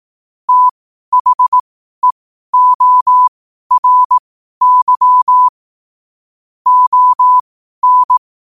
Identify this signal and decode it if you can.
Morse code, heard as a keyed tone, about 18 wpm, spaced as for 12 wpm, 1000 Hz. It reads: THEORY ON
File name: qotdfiles / 12wpm